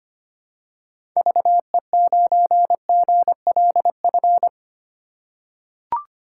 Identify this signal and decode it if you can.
Morse code, heard as a keyed tone, about 25 wpm, 700 Hz. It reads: VE9GLF
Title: K1USN SST QSO Practice - 25wpm